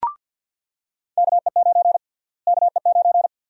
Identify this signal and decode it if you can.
Morse code, heard as a keyed tone, about 50 wpm, 700 Hz. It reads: KE9 KE9
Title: USA Call Sign Prefixes - Repeated 2x - 50wpm